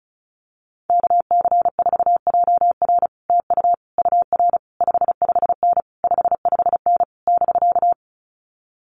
Morse code, heard as a keyed tone, about 34 wpm, 700 Hz. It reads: KC4JR TU UR 55N 55N <BK>
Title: Common QSO Phrases Encore 35wpm